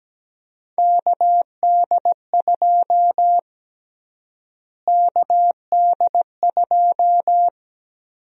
Morse code, heard as a keyed tone, about 17 wpm, 700 Hz. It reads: KD2 KD2
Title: USA Call Sign Prefixes - Repeated 2x - 17wpm